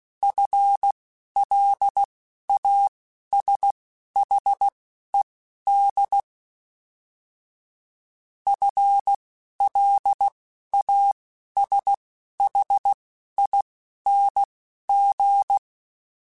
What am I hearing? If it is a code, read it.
Morse code, heard as a keyed tone, about 16 wpm, spaced as for 11 wpm, 800 Hz. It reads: FLASHED FLASHING
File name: FL Runs / FL Runs mp3s / FL Runs @10